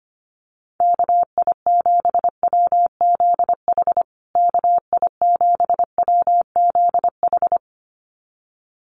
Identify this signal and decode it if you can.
Morse code, heard as a keyed tone, about 25 wpm, 700 Hz. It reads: KI7WZH KI7WZH
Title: USA Call Signs Repeated 2x 25wpm